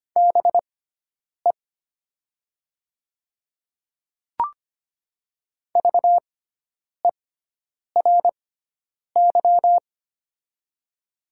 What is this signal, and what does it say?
Morse code, heard as a keyed tone, about 25 wpm, spaced as for 9 wpm, 700 Hz. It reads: BE VERY
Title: 6x Character Spacing - Top 200 Words - 25wpm